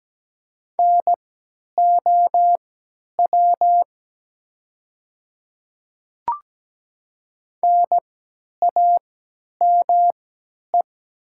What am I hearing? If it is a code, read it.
Morse code, heard as a keyed tone, about 17 wpm, spaced as for 10 wpm, 700 Hz. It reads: NOW NAME
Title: 3x Character Spacing - Top 200 Words - 17wpm